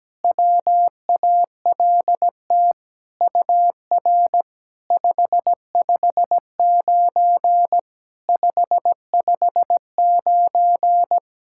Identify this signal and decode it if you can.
Morse code, heard as a keyed tone, about 17 wpm, 700 Hz. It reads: WALT UR 559 559
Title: SOTA QSO Practice 17wpm